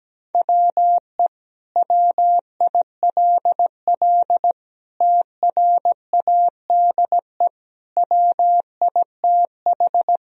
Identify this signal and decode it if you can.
Morse code, heard as a keyed tone, about 17 wpm, 700 Hz. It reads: WE WILL TRADE WITH
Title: Sentences from Top 1000 Words - Review Part 4 - 17wpm